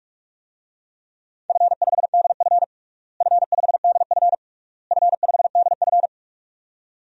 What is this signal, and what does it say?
Morse code, heard as a keyed tone, about 46 wpm, 700 Hz. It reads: F5DF F5DF F5DF
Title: DXCC-100 Call Signs - 4 Characters - Repeated 3x - 45wpm